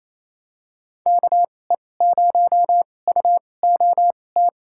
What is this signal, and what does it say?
Morse code, heard as a keyed tone, about 28 wpm, spaced as for 20 wpm, 700 Hz. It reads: KE0UOT
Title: USA Call Signs - 6 Characters - 2x Character Spacing - 28wpm